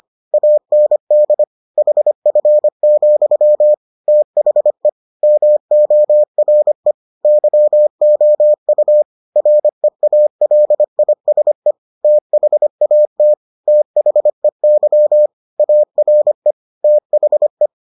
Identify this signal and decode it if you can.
Morse code, heard as a keyed tone, about 25 wpm, 600 Hz. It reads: AND HF, THE MORE YOU REALISE THAT THEY ARE THE